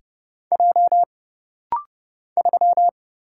Morse code, heard as a keyed tone, about 30 wpm, 700 Hz. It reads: J 3
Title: Single Letter-Number Alt - Rapid-Fire 30wpm